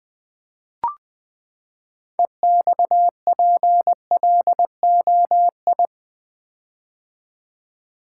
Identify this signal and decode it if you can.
Morse code, heard as a keyed tone, about 20 wpm, 700 Hz. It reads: EXPLOI
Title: Build A Word - 6 Letters At A Time - 12-Letter Words - 20wpm